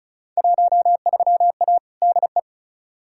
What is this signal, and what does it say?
Morse code, heard as a keyed tone, about 36 wpm, 700 Hz. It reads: 13A DE
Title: Field Day Exchanges Encore - 35wpm